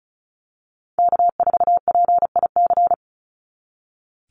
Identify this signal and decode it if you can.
Morse code, heard as a keyed tone, about 36 wpm, 700 Hz. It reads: K4PIC
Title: NA SOTA Chaser Callsigns 35wpm